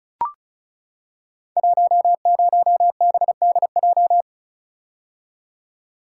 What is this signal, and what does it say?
Morse code, heard as a keyed tone, about 36 wpm, 690 Hz. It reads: 10BDJ
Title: Code Groups - 5 Letter:Numbers - 35wpm